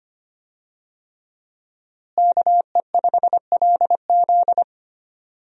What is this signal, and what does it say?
Morse code, heard as a keyed tone, about 25 wpm, 700 Hz. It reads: KE5LZ
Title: USA Call Signs - 5 Characters - Repeated 3x - 25wpm